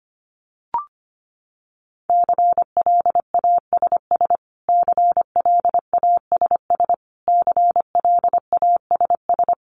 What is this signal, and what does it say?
Morse code, heard as a keyed tone, about 25 wpm, 700 Hz. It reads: CLASS CLASS CLASS
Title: Top 500 Words Repeated 25wpm